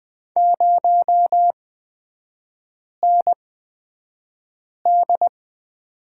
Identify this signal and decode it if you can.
Morse code, heard as a keyed tone, about 20 wpm, spaced as for 5 wpm, 700 Hz. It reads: 0ND
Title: 250 Lesson 40e - BK - Three Characters 20wpm